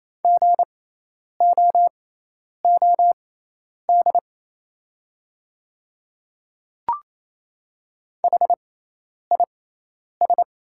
Morse code, heard as a keyed tone, about 28 wpm, spaced as for 10 wpm, 700 Hz. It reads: GOOD HIS
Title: 6x Character Spacing - Top 200 Words - 28wpm